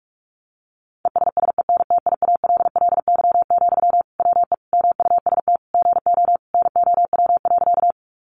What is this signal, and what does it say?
Morse code, heard as a keyed tone, about 46 wpm, 700 Hz. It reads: ESSENTIALLY, WE MUST GO NOW.
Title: Sentences from Top 1300 Words 45wpm